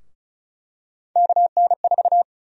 Morse code, heard as a keyed tone, about 36 wpm, 700 Hz. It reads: KN4
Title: USA Call Sign Prefixes - Repeated 2x - 35wpm